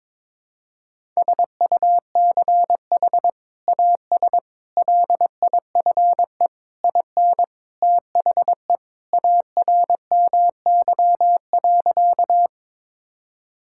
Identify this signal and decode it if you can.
Morse code, heard as a keyed tone, about 22 wpm, 700 Hz. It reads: SUCH AS LIFE IN THE ARMY.
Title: Sentences from Top 1000 Words - Review Part 5 - 22wpm